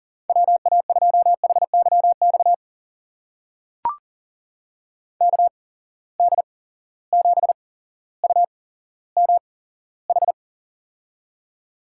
Morse code, heard as a keyed tone, about 40 wpm, spaced as for 17 wpm, 700 Hz. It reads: WA2HYX KD7UMH
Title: USA Call Signs - 6 Characters - 8x Character Spacing - 40wpm